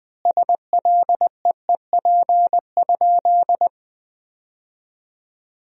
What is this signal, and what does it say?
Morse code, heard as a keyed tone, about 20 wpm, 700 Hz. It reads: SLEEP?
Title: 4x Word-Spacing - Sentences from Top 500 Words - 20wpm